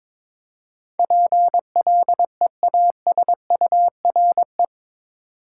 Morse code, heard as a keyed tone, about 22 wpm, 700 Hz. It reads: PLEASURE
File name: practice / 22wpm / Eight Letter Words - 22wpm